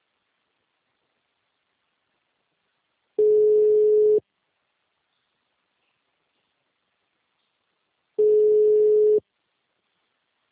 Ringback tone (Europe, 425 Hz).
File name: out- -2034- - - .7276